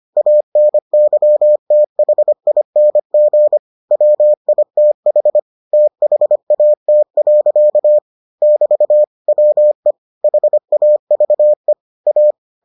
Morse code, heard as a keyed tone, about 25 wpm, 600 Hz. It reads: ANYTHING WITH THAT. <BT> WE HAVE A